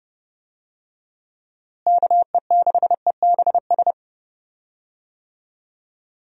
Morse code, heard as a keyed tone, about 29 wpm, 700 Hz. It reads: KE6EBS
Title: USA Call Signs - 6 Characters - Repeated 3x - 30wpm